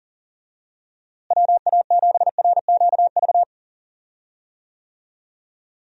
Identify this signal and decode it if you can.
Morse code, heard as a keyed tone, about 40 wpm, 700 Hz. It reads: WA7RQV
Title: USA Call Signs - 6 Characters - Repeated 3x - 40wpm